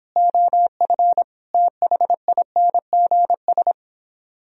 Morse code, heard as a keyed tone, about 31 wpm, 700 Hz. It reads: OF THINGS